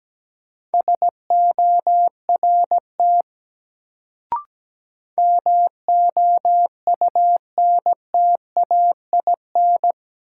Morse code, heard as a keyed tone, about 17 wpm, 700 Hz. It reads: SORT MOUNTAIN
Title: Sentences from Top 1300 Words - New Word Drill Alt - 17wpm